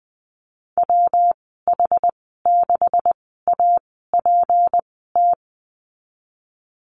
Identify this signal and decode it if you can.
Morse code, heard as a keyed tone, about 20 wpm, spaced as for 15 wpm, 700 Hz. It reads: WH6APT
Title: 173 Lesson 28g - Number 9 - Call Signs 20wpm